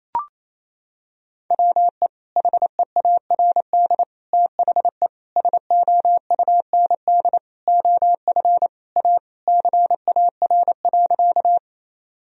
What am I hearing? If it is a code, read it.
Morse code, heard as a keyed tone, about 28 wpm, 700 Hz. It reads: WE HEARD THE SOUND OF A CAR.
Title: Sentences from Top 1000 Words - Review Part 1 - 28wpm